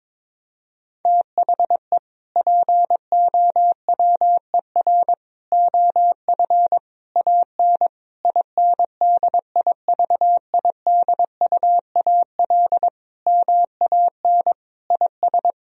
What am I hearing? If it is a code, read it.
Morse code, heard as a keyed tone, about 22 wpm, 700 Hz. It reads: THE POWER OF AN INDIVIDUAL MAN IS
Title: Sentences from Top 1000 Words - Review Part 4 - 22wpm